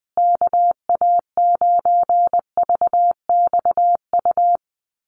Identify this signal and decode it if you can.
Morse code, heard as a keyed tone, about 20 wpm, 700 Hz. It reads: KA9VXU
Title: USA Call Signs Repeated 2x 20wpm